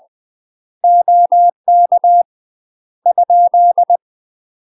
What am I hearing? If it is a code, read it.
Morse code, heard as a keyed tone, about 20 wpm, 700 Hz. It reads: OK ?
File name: UK MASTER QSOs TRAINING / 20WPM / UK MASTER QSOs 0004